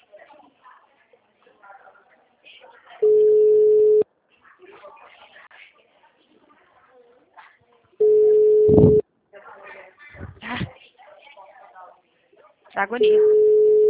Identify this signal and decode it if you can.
Ringback tone (Europe, 425 Hz), with speech.
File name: out- -2006- - - .258